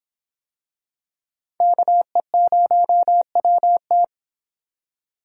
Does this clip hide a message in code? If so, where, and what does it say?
Yes: Morse code from 1.6 to 4.0 s, heard as a keyed tone, about 24 wpm, 700 Hz. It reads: KE0WT